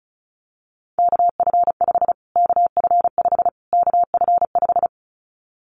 Morse code, heard as a keyed tone, about 34 wpm, 700 Hz. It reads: KF5 KF5 KF5
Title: Callsign Prefixes for Top 25 DXCC - Repeated 3x - 35wpm